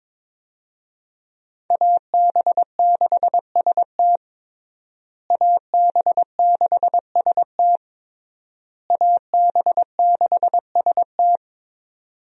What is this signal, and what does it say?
Morse code, heard as a keyed tone, about 22 wpm, 700 Hz. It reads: AB6ST AB6ST AB6ST
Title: USA Call Signs - 5 Characters - Repeated 3x - 22wpm